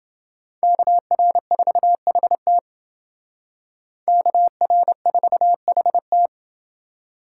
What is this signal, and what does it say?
Morse code, heard as a keyed tone, about 27 wpm, 700 Hz. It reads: KR4HT KR4HT